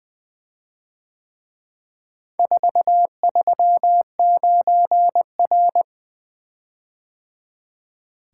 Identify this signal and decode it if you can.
Morse code, heard as a keyed tone, about 24 wpm, 700 Hz. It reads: 439R
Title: Speed-Racing - Four Letter-Number Combinations - 20wpm